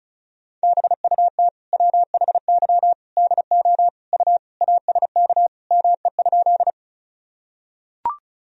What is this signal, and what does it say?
Morse code, heard as a keyed tone, about 36 wpm, 700 Hz. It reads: BUT WHY DO U ASK ME?
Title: Sentences from Top 200 Words and Abbreviations 35wpm